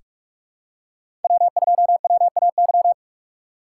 Morse code, heard as a keyed tone, about 46 wpm, 700 Hz. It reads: W2WAY
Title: USA Call Signs - 5 Characters - 6x Character Spacing - 45wpm